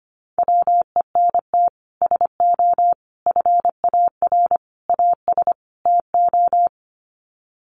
Morse code, heard as a keyed tone, about 25 wpm, 700 Hz. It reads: WENT SO FAR AS TO
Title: Sets of 5 Words Encore 25wpm